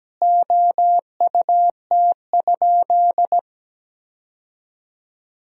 Morse code, heard as a keyed tone, about 17 wpm, 700 Hz. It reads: OUT?
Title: 4x Word-Spacing - Sentences from Top 300 Words - 17wpm